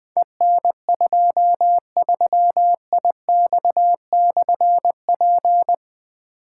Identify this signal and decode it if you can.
Morse code, heard as a keyed tone, about 20 wpm, 700 Hz. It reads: EN23IX/P
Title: Difficult Call Signs 20wpm